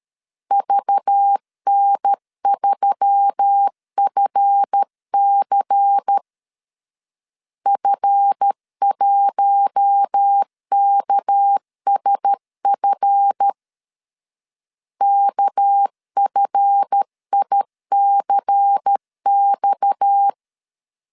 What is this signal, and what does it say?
Morse code, heard as a keyed tone, about 13 wpm, 800 Hz. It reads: VN3FC F1KSF KFICX
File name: LEZIONI / 7 GF9Z / 7 GF 12wpm 2122